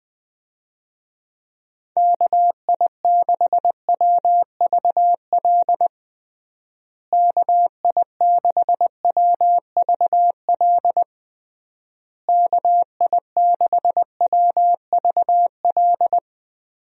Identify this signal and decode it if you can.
Morse code, heard as a keyed tone, about 20 wpm, 700 Hz. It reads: KI6WVL KI6WVL KI6WVL
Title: USA Call Signs - 6 Characters - Repeated 3x - 20wpm